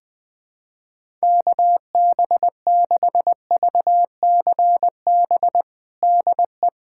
Morse code, heard as a keyed tone, about 21 wpm, 700 Hz. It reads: KB6VCB DE
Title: USA Callsign DE Callsign 20wpm